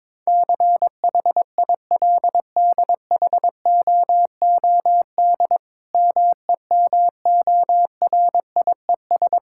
Morse code, heard as a keyed tone, about 22 wpm, 700 Hz. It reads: CHILDHOOD MEMORIES
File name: Difficult Sentences with Hints - 22wpm